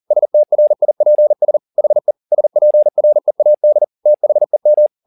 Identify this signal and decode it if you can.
Morse code, heard as a keyed tone, about 40 wpm, 600 Hz. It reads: STRIPS HE SPREAD THEM